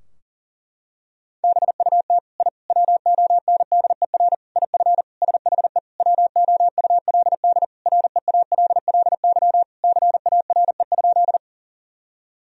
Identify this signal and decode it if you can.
Morse code, heard as a keyed tone, about 37 wpm, 700 Hz. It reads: BUT I WONDER IF SHE WOULD REALLY CARE?
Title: Sentences from Top 1000 Words - Review Part 3 - 40wpm